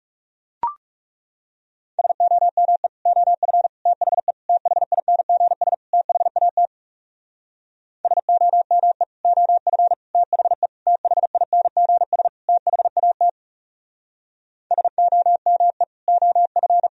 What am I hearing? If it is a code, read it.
Morse code, heard as a keyed tone, about 40 wpm, 700 Hz. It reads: SOME OF THE THINGS THAT SOME OF THE THINGS THAT SOME OF